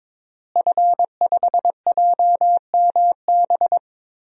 Morse code, heard as a keyed tone, about 22 wpm, 700 Hz. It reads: F5JMB
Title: DXCC-100 Call Signs - 5 Characters - Repeated 2x - 22wpm